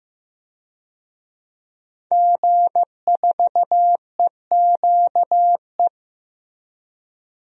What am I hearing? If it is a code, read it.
Morse code, heard as a keyed tone, about 15 wpm, 700 Hz. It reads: G4EQE
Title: DXCC-100 Call Signs - 5 Characters - Repeated 2x - 15wpm